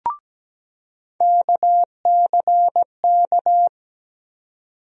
Morse code, heard as a keyed tone, about 17 wpm, 700 Hz. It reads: KCK
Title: USA Call Sign Postfixes - 3 Letters 17wpm